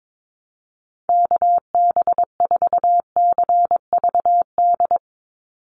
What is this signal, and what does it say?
Morse code, heard as a keyed tone, about 22 wpm, 700 Hz. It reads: KB4CVD
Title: USA Call Signs 22wpm